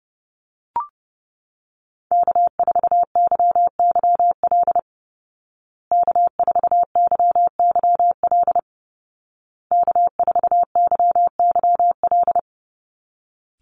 Morse code, heard as a keyed tone, about 30 wpm, 700 Hz. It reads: K4YYL K4YYL K4YYL
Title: Call Signs Repeated 2x - 30wpm